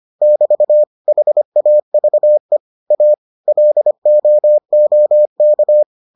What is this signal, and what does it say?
Morse code, heard as a keyed tone, about 25 wpm, 600 Hz. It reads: <BT> HAVE A LOOK